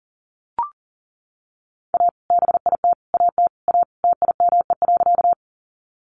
Morse code, heard as a keyed tone, about 37 wpm, 700 Hz. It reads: A BIT AT A TIME.
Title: Sentences from Top 900 Words 40wpm